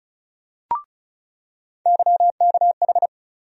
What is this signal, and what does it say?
Morse code, heard as a keyed tone, about 36 wpm, 700 Hz. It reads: YKH
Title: USA Call Sign Postfixes - 3 Letters 35wpm